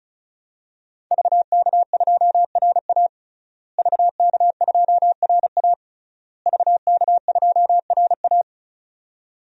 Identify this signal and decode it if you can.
Morse code, heard as a keyed tone, about 36 wpm, 700 Hz. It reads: VK2RA VK2RA VK2RA
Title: DXCC-100 Call Signs - 5 Characters - Repeated 3x - 35wpm